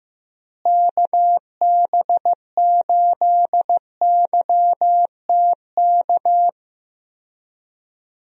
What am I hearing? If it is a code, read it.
Morse code, heard as a keyed tone, about 15 wpm, 700 Hz. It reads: KB8YTK
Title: USA Call Signs - 6 Characters - Repeated 2x - 15wpm